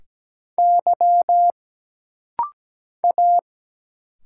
Morse code, heard as a keyed tone, about 17 wpm, 700 Hz. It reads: Y A
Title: Single Letter-Number Alt - Rapid-Fire 17wpm